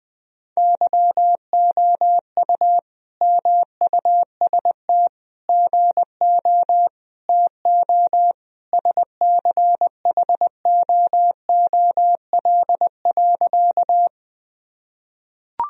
Morse code, heard as a keyed tone, about 20 wpm, 700 Hz. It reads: YOU MUST GO TO SCHOOL.
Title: Sentences from Top 1000 Words - Review Part 1 - 20wpm